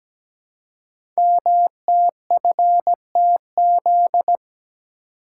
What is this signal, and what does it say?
Morse code, heard as a keyed tone, about 17 wpm, 700 Hz. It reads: MTFTZ
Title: Code Groups - 5 Letter:Numbers - 17wpm